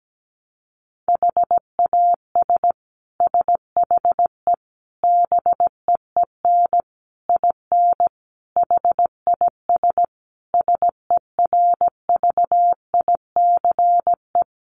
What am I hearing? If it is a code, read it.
Morse code, heard as a keyed tone, about 17 wpm, 700 Hz. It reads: HAS SHE BEEN IN HIS SERVICE
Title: Sentences from Top 700 Words 17wpm